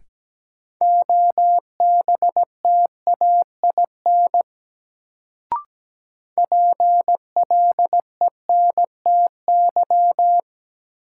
Morse code, heard as a keyed tone, about 17 wpm, 700 Hz. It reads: OBTAIN PLENTY
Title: Sentences from Top 1300 Words - New Word Drill - 17wpm